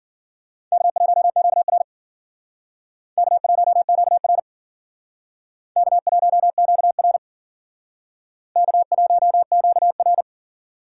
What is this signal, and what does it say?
Morse code, heard as a keyed tone, about 52 wpm, 700 Hz. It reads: K1QR K1QR K1QR K1QR